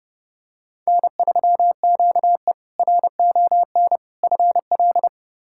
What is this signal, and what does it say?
Morse code, heard as a keyed tone, about 30 wpm, 700 Hz. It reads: N3QE RON FL
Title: K1USN SST QSO Practice - 30wpm